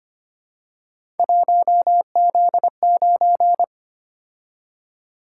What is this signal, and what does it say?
Morse code, heard as a keyed tone, about 25 wpm, 700 Hz. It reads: 1Z9
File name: Three Letter-Number Combinations - 25wpm